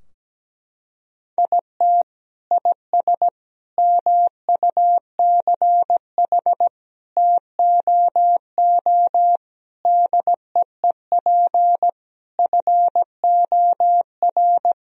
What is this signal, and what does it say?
Morse code, heard as a keyed tone, about 17 wpm, 700 Hz. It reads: IT IS MUCH TOO DEEP FOR
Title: Sentences from Top 1000 Words - Review Part 3 - 17wpm